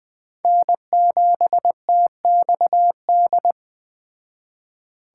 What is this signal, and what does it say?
Morse code, heard as a keyed tone, about 20 wpm, 700 Hz. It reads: N7TXD